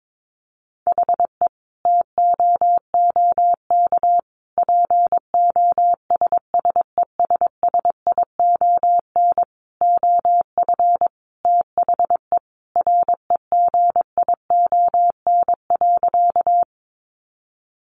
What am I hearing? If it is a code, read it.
Morse code, heard as a keyed tone, about 22 wpm, 700 Hz. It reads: HE TOOK POSSESSION OF THE REGION.
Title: Sentences from Top 1500 Words 22wpm